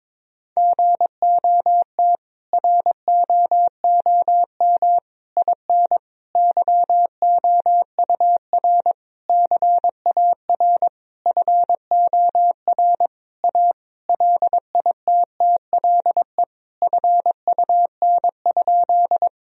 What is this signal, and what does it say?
Morse code, heard as a keyed tone, about 22 wpm, 700 Hz. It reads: GOT ROOM IN YOUR CAR FOR A LITTLE FUN?
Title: Sentences from Top 1000 Words - Review Part 1 - 22wpm